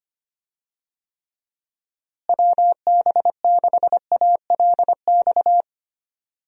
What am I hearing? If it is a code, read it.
Morse code, heard as a keyed tone, about 25 wpm, 700 Hz. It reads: WB6ALX
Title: USA Call Signs - 6 Characters - Repeated 3x - 25wpm